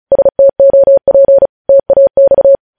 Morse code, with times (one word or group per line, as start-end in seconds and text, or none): 0.1-1.5 STOP
1.7-2.6 TAX